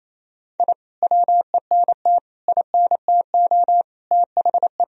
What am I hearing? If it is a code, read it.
Morse code, heard as a keyed tone, about 28 wpm, 700 Hz. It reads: I WENT INTO THE
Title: Sentences from Top 1000 Words - Review Part 1 - 28wpm